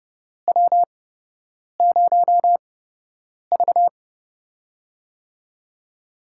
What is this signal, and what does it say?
Morse code, heard as a keyed tone, about 29 wpm, spaced as for 8 wpm, 700 Hz. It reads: W0V
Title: USA Call Signs - 4 Characters - 8x Character Spacing - 30wpm